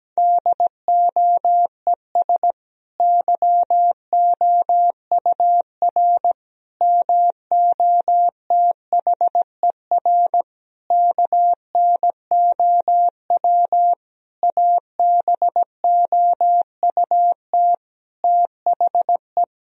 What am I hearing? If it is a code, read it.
Morse code, heard as a keyed tone, about 17 wpm, 700 Hz. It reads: DOES YOUR MOTHER KNOW ABOUT THE
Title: Sentences from Top 1000 Words - Review Part 4 - 17wpm